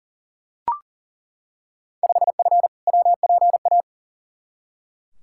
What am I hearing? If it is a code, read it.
Morse code, heard as a keyed tone, about 37 wpm, 700 Hz. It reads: 5F WPA
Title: Field Day Exchanges Encore - 40wpm